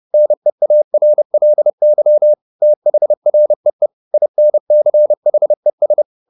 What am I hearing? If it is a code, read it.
Morse code, heard as a keyed tone, about 30 wpm, 600 Hz. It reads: NEARLY THREE INCHES
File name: the old man and the sea 2 30wpm